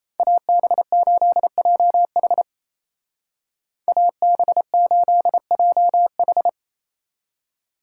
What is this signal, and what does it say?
Morse code, heard as a keyed tone, about 34 wpm, 700 Hz. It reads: AB8JH AB8JH
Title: Speed Racing - USA 2x2 Call Signs 28wpm